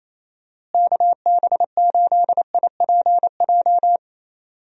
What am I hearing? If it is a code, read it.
Morse code, heard as a keyed tone, about 28 wpm, 700 Hz. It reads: KB8IPJ
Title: USA Call Signs - 6 Characters - Repeated 3x - 28wpm